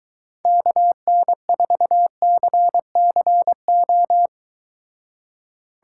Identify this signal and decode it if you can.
Morse code, heard as a keyed tone, about 23 wpm, 700 Hz. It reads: KN4CCO